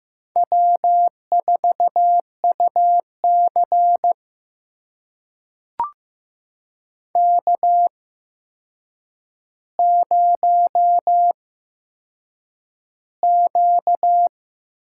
Morse code, heard as a keyed tone, about 15 wpm, spaced as for 7 wpm, 700 Hz. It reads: W4UC K0Q
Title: USA Call Signs - 4 Characters - 8x Character Spacing - 15wpm